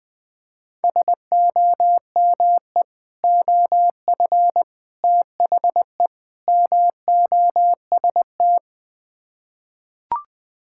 Morse code, heard as a keyed tone, about 20 wpm, 700 Hz. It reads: SOME OF THE MOST
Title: Build A Phrase - 4-Word Phrases - 20wpm